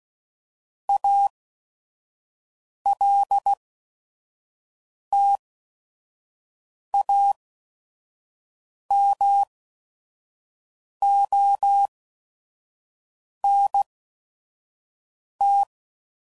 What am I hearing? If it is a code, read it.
Morse code, heard as a keyed tone, about 16 wpm, spaced as for 5 wpm, 800 Hz. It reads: ALTAMONT
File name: CW Code Groups @5WPM / fromSM's / Cities